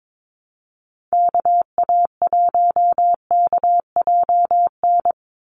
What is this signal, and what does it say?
Morse code, heard as a keyed tone, about 22 wpm, 700 Hz. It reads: KA1KJN
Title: USA Call Signs 22wpm